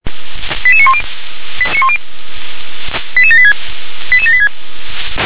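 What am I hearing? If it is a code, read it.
Five-tone selective call (ZVEI), heard as quick runs of five tones, tones of about 70 ms. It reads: 88910 88786 88786
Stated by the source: Five-tone ZVEI selective call received by SDR.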